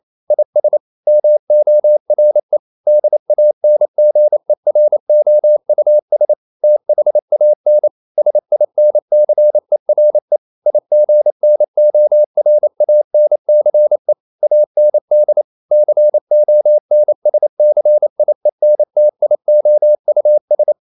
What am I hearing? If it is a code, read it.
Morse code, heard as a keyed tone, about 28 wpm, 600 Hz. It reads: IS MORE DANGEROUS THAN SINCERE IGNORANCE AND CONSCIENTIOUS
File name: CW / Practice / EnglishQuotations0128